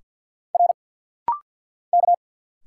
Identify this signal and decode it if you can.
Morse code, heard as a keyed tone, about 50 wpm, 690 Hz. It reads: R K
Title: Single Letters - Rapid-Fire 50wpm